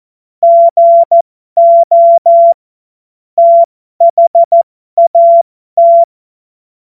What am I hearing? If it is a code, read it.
Morse code, heard as a keyed tone, about 14 wpm, spaced as for 12 wpm, 680 Hz. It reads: GO THAT
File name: Morsekurs / Witze